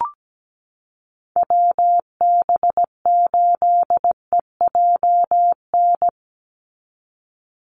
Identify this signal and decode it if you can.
Morse code, heard as a keyed tone, about 17 wpm, 700 Hz. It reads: WB8EJN
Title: Call Signs - 17wpm